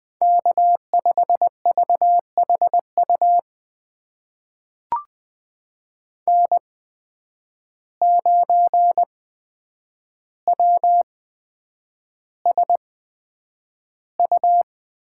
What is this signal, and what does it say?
Morse code, heard as a keyed tone, about 20 wpm, spaced as for 9 wpm, 700 Hz. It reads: K5VHU N9WSU
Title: USA Call Signs - 8x Character Spacing - 20wpm